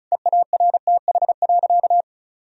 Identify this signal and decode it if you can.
Morse code, heard as a keyed tone, about 36 wpm, 700 Hz. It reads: EARTH.
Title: Sentences from Top 1000 Words - Review Part 2 - 35wpm